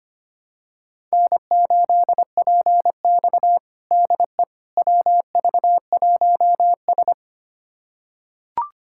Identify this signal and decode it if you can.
Morse code, heard as a keyed tone, about 25 wpm, 700 Hz. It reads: N8PX DE WV1S
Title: USA Callsign DE Callsign - 4 Characters - 25wpm